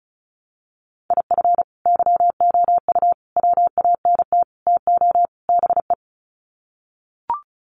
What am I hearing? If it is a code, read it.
Morse code, heard as a keyed tone, about 34 wpm, 700 Hz. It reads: IF YOU WANT TO BE
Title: Sets of 5 Words Encore 35wpm